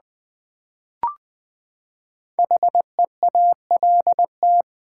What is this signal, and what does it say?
Morse code, heard as a keyed tone, about 20 wpm, 700 Hz. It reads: HEALT
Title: Build A Word - 5 Letters At A Time - 10-Letter Words - 20wpm